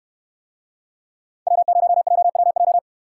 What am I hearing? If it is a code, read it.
Morse code, heard as a keyed tone, about 65 wpm, 700 Hz. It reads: W0JWJ